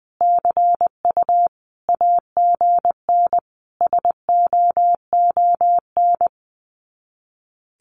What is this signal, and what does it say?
Morse code, heard as a keyed tone, about 20 wpm, 700 Hz. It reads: CU AGN SOON
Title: Common QSO Phrases 20wpm